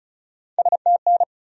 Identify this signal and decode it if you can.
Morse code, heard as a keyed tone, about 48 wpm, spaced as for 42 wpm, 700 Hz. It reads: STN
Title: Speed-Racing Abbreviations - 35wpm